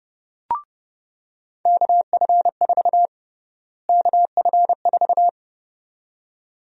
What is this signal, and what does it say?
Morse code, heard as a keyed tone, about 32 wpm, 700 Hz. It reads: KF4 KF4
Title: USA Call Sign Prefixes - Repeated 2x - 30wpm